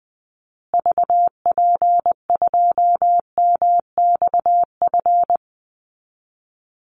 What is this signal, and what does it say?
Morse code, heard as a keyed tone, about 20 wpm, 700 Hz. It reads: VP2MXF
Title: Call Signs Repeated 3x - 20wpm